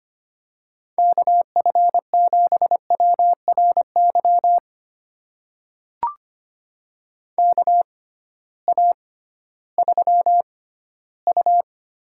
Morse code, heard as a keyed tone, about 25 wpm, spaced as for 15 wpm, 700 Hz. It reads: KF7WRY KA3U
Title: USA Call Signs - 6 Characters - 6x Character Spacing - 25wpm